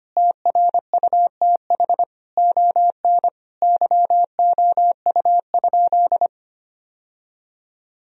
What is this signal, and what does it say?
Morse code, heard as a keyed tone, about 25 wpm, 700 Hz. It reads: TRUTH ON YOU?
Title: Sentences from Top 1000 Words - Review Part 3 - 25wpm